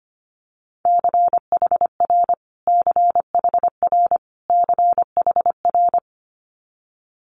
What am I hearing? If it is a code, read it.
Morse code, heard as a keyed tone, about 25 wpm, 700 Hz. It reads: CHR CHR CHR
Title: Multi-letter Phonemes Repeated 3x 25wpm